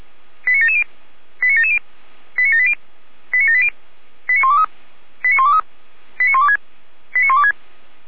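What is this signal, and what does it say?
Five-tone selective call (ZVEI), heard as quick runs of five tones, tones of about 70 ms. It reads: 89770 89770 89780 89780 89123 89123 89126 89126
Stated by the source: ZVEI five-tone selective call received on an SDR.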